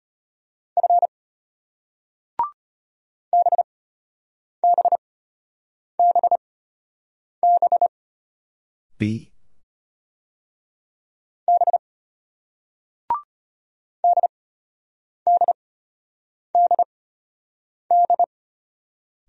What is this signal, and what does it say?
Morse code, heard as a keyed tone, about 33 wpm, spaced as for 8 wpm, 690 Hz. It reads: F BBBB B DDDD